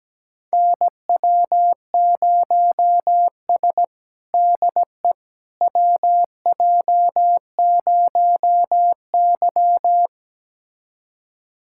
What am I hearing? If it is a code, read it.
Morse code, heard as a keyed tone, about 17 wpm, 700 Hz. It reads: NW0S DE WJ0Y
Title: USA Callsign DE Callsign - 4 Characters - 17wpm